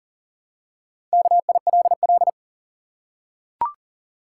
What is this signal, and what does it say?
Morse code, heard as a keyed tone, about 40 wpm, 700 Hz. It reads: KILL
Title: Four Letter Words - 40wpm